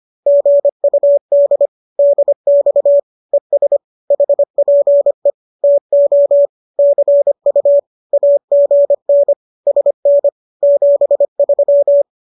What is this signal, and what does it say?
Morse code, heard as a keyed tone, about 25 wpm, 570 Hz. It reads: GUD DX ES HPE TO CU AGN SN 73